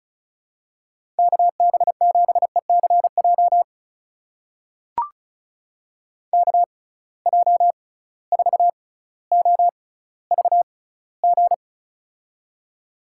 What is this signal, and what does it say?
Morse code, heard as a keyed tone, about 36 wpm, spaced as for 18 wpm, 700 Hz. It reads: KB7ECJ KJ4OVG
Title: USA Call Signs - 6 Characters - 6x Character Spacing - 35wpm